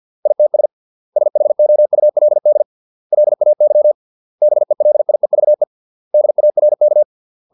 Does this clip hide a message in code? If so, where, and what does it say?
Yes: Morse code from 0.3 to 7.0 s, heard as a keyed tone, about 50 wpm, 600 Hz. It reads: ITS SHOULD LAY BELIEVE DARK